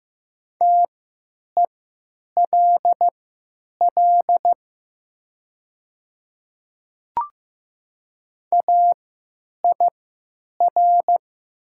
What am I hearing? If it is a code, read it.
Morse code, heard as a keyed tone, about 15 wpm, spaced as for 9 wpm, 700 Hz. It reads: TELL AIR
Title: 3x Character Spacing - Top 200 Words - 15wpm